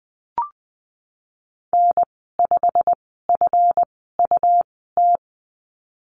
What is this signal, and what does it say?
Morse code, heard as a keyed tone, about 20 wpm, spaced as for 15 wpm, 700 Hz. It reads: N5FUT
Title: 147 Lesson 24g - Letter y - Call Signs 20wpm